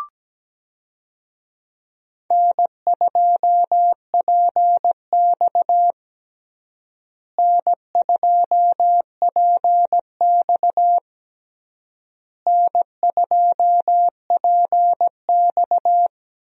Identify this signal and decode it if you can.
Morse code, heard as a keyed tone, about 17 wpm, 700 Hz. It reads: N2PX N2PX N2PX
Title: USA Call Signs - 4 Characters - Repeated 3x - 17wpm